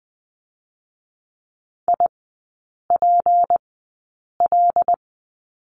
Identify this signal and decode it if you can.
Morse code, heard as a keyed tone, about 20 wpm, spaced as for 8 wpm, 700 Hz. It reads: IPL
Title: 178 Lesson 29e - Slash - Three Characters 20wpm